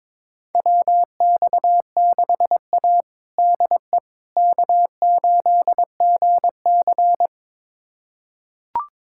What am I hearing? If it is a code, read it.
Morse code, heard as a keyed tone, about 22 wpm, 700 Hz. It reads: WX6A DE K8GC
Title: USA Callsign DE Callsign - 4 Characters - 22wpm